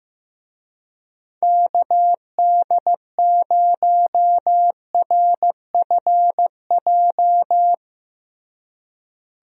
Morse code, heard as a keyed tone, about 15 wpm, 700 Hz. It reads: KD0RFJ
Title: USA Call Signs - 6 Characters - Repeated 3x - 15wpm